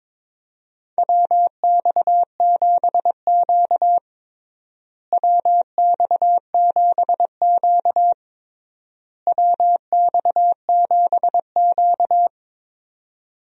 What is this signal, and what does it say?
Morse code, heard as a keyed tone, about 22 wpm, 700 Hz. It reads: WX7Q WX7Q WX7Q
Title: USA Call Signs - 4 Characters - Repeated 3x - 22wpm